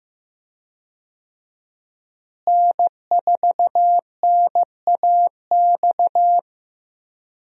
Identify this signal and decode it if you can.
Morse code, heard as a keyed tone, about 15 wpm, 700 Hz. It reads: N4NAX
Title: USA Call Signs - 5 Characters - Repeated 3x - 15wpm